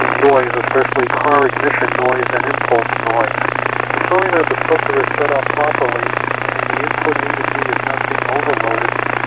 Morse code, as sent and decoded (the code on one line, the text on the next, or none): none